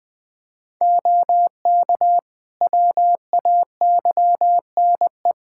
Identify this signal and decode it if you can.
Morse code, heard as a keyed tone, about 20 wpm, 700 Hz. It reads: OK WAYNE
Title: Practice QSO - Exchange 3 - More Info - 20wpm